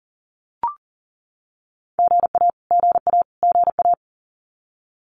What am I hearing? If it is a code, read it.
Morse code, heard as a keyed tone, about 40 wpm, 700 Hz. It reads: GA GA GA
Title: Abbreviations Repeated 3x - 40wpm